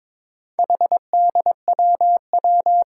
Morse code, heard as a keyed tone, about 22 wpm, 700 Hz. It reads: HDWW
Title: Four Challenging Letter-Number Combinations 22wpm